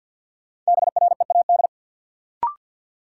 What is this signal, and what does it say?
Morse code, heard as a keyed tone, about 53 wpm, 700 Hz. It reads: BREAD
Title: Sentences from Top 1600 Words - New Word Drill - 50wpm